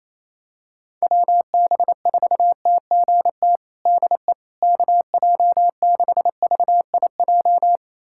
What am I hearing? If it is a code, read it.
Morse code, heard as a keyed tone, about 28 wpm, 700 Hz. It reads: WB4TGT DE KJ6VIJ
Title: USA Callsign DE Callsign - 6 Characters - 28wpm